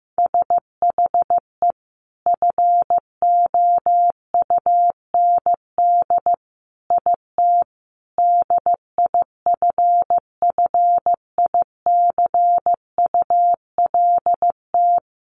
Morse code, heard as a keyed tone, about 15 wpm, 700 Hz. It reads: SHE FOUND IT DIFFICULT
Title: Sentences from Top 1500 Words 15wpm